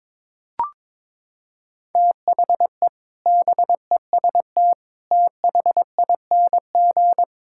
Morse code, heard as a keyed tone, about 22 wpm, 700 Hz. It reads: THE BEST THING
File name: practice / 22wpm / Sets of 7 Words - 22wpm